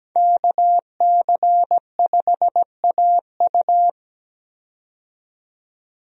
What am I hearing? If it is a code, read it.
Morse code, heard as a keyed tone, about 17 wpm, 700 Hz. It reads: KC5AU
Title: USA Call Signs - 5 Characters - 17wpm